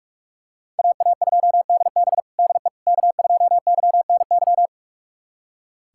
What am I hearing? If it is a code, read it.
Morse code, heard as a keyed tone, about 41 wpm, 700 Hz. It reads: AA2DB DE K2YNY
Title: USA Callsign DE Callsign - 5 Characters - 45wpm